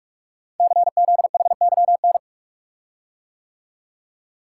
Morse code, heard as a keyed tone, about 44 wpm, 690 Hz. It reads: CZHYN
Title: Code Groups - 5 Letters - 45wpm